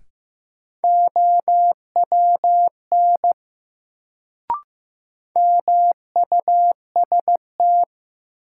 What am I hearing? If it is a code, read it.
Morse code, heard as a keyed tone, about 15 wpm, 700 Hz. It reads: OWN MUST
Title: Sets of 2 Words - New Word Drill - 15wpm